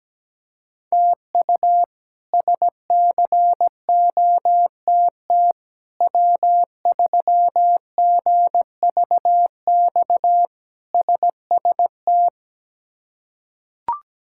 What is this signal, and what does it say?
Morse code, heard as a keyed tone, about 17 wpm, 700 Hz. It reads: TU SCOTT W3GVX SST
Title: K1USN SST QSO Practice - 17wpm